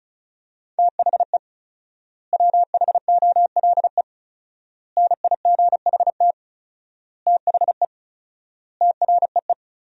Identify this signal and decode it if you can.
Morse code, heard as a keyed tone, about 36 wpm, 700 Hz. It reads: THE WHOLE NIGHT THE TREE
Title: 4x Word-Spacing - Sentences from Top 500 Words - 35wpm